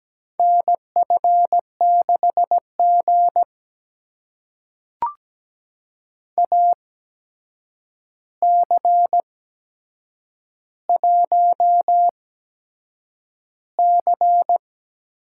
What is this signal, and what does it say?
Morse code, heard as a keyed tone, about 17 wpm, spaced as for 7 wpm, 700 Hz. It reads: NF6G AC1C
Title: USA Call Signs - 4 Characters - 8x Character Spacing - 17wpm